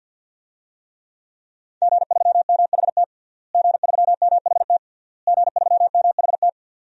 Morse code, heard as a keyed tone, about 49 wpm, 700 Hz. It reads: G3MHT G3MHT G3MHT
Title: DXCC-100 Call Signs - 5 Characters - Repeated 3x - 50wpm